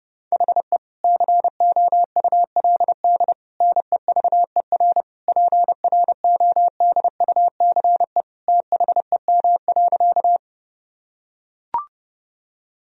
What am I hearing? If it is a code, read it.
Morse code, heard as a keyed tone, about 32 wpm, 700 Hz. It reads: HE COULD NEVER PRODUCE THEM.
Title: Sentences from Top 1000 Words - Review Part 3 - 30wpm